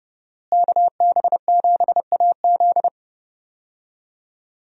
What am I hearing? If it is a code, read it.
Morse code, heard as a keyed tone, about 30 wpm, 700 Hz. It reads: KB7AZ
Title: USA Call Signs - 5 Characters - Repeated 3x - 30wpm